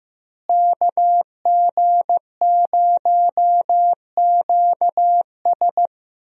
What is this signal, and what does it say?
Morse code, heard as a keyed tone, about 15 wpm, 700 Hz. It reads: KG0QS
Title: USA Call Signs - 5 Characters - Repeated 2x - 15wpm